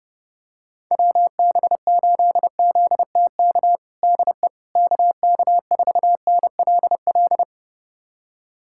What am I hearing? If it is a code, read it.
Morse code, heard as a keyed tone, about 30 wpm, 700 Hz. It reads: WB8ZTK DE KK4NLL
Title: USA Callsign DE Callsign - 6 Characters - 30wpm